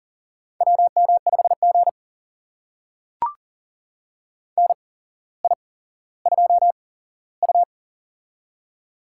Morse code, heard as a keyed tone, about 40 wpm, spaced as for 17 wpm, 700 Hz. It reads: WM5G NI2U
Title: USA Call Signs - 4 Characters - 8x Character Spacing - 40wpm